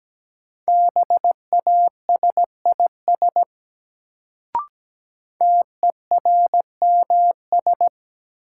Morse code, heard as a keyed tone, about 17 wpm, 700 Hz. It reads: BASIS TERMS
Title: Sets of 3 Words - New Word Drill - 17wpm